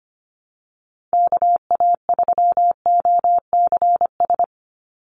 Morse code, heard as a keyed tone, about 25 wpm, 700 Hz. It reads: KA3OCS
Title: Call Signs - 25wpm